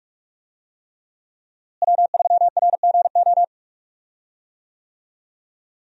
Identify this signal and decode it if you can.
Morse code, heard as a keyed tone, about 46 wpm, 700 Hz. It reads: W3RGO
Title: USA Call Signs - 5 Characters - Repeated 3x - 45wpm